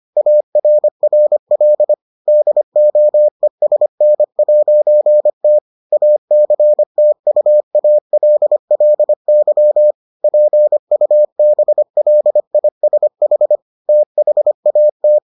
Morse code, heard as a keyed tone, about 25 wpm, 600 Hz. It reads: ARRL DOESN'T ACTUALLY PUBLISH THAT